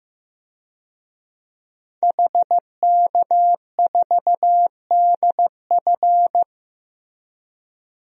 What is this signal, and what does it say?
Morse code, heard as a keyed tone, about 15 wpm, 700 Hz. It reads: HK4DF
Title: DXCC-100 Call Signs - 5 Characters - Repeated 2x - 15wpm